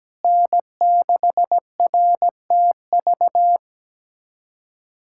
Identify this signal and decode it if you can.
Morse code, heard as a keyed tone, about 17 wpm, 700 Hz. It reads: N6RTV
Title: USA Callsign DE Callsign 17wpm